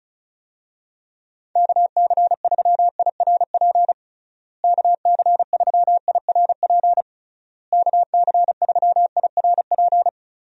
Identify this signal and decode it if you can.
Morse code, heard as a keyed tone, about 36 wpm, 700 Hz. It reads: KC3IRP KC3IRP KC3IRP
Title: USA Call Signs - 6 Characters - Repeated 3x - 35wpm